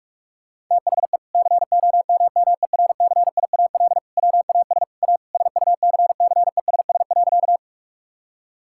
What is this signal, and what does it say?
Morse code, heard as a keyed tone, about 47 wpm, 700 Hz. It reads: THE COMMERCIAL WAS A SUCCESS.
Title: Sentences from Top 1000 Words - Review Part 5 - 45wpm